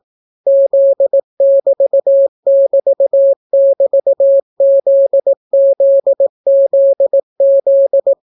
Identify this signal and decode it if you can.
Morse code, heard as a keyed tone, about 18 wpm, 550 Hz. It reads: Z<BT><BT><BT>ZZZZ